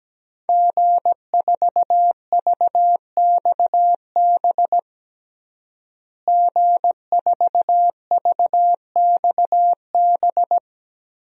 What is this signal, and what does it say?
Morse code, heard as a keyed tone, about 17 wpm, 700 Hz. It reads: G4VXB G4VXB
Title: DXCC-100 Call Signs - 5 Characters - Repeated 2x - 17wpm